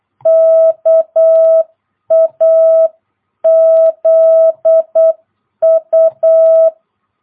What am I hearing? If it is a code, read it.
Morse code, heard as a keyed tone, about 8 wpm, 660 Hz. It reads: KAZU
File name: jh8cbh / 2022cw